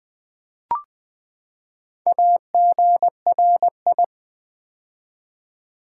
Morse code, heard as a keyed tone, about 20 wpm, 700 Hz. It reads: AGRI
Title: Build A Word - 4 Letters At A Time - 11-Letter Words - 20wpm